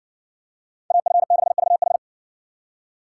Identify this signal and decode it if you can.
Morse code, heard as a keyed tone, about 56 wpm, 700 Hz. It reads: AF6VH